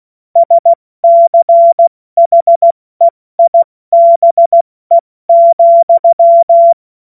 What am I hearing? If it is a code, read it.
Morse code, heard as a keyed tone, about 16 wpm, spaced as for 14 wpm, 680 Hz. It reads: SCHEIBE,
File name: Morsekurs / Witze / Witz 26. Oktober 2021, 70 F80